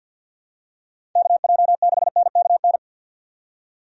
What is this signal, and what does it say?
Morse code, heard as a keyed tone, about 53 wpm, 700 Hz. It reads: KJ6NKN
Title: USA Call Signs - 6 Characters - 50wpm